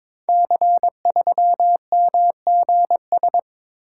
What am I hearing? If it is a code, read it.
Morse code, heard as a keyed tone, about 22 wpm, 700 Hz. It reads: C3MGS
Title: Code Groups - 5 Letter:Numbers - 22wpm